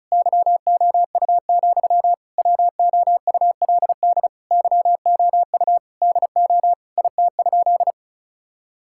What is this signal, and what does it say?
Morse code, heard as a keyed tone, about 36 wpm, 700 Hz. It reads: YOU, WOULD YOU DO IT?
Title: Sentences from Top 1000 Words - Review Part 5 - 35wpm